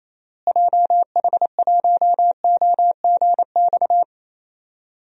Morse code, heard as a keyed tone, about 28 wpm, 700 Hz. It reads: JH1OGX
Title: DXCC-100 Call Signs - 6 Characters - Repeated 3x - 28wpm